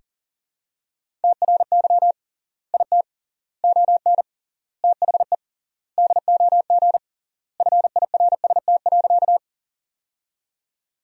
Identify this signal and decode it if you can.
Morse code, heard as a keyed tone, about 40 wpm, 700 Hz. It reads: TRY IT ON THE DOG FIRST.
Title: 3x Word-Spacing - Sentences from Top 400 Words - 40wpm